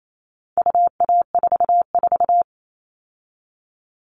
Morse code, heard as a keyed tone, about 28 wpm, 700 Hz. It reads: UA44
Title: All Cut Numbers - 4-Digit 28wpm